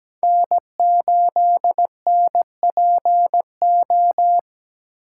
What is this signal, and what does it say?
Morse code, heard as a keyed tone, about 20 wpm, 700 Hz. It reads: N8NPO
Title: Speed Racing - USA 1x3 Call Signs 17wpm